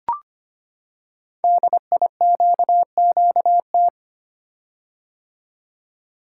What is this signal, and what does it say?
Morse code, heard as a keyed tone, about 25 wpm, 700 Hz. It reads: DIQQT
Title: Code Groups - 5 Letters - 25wpm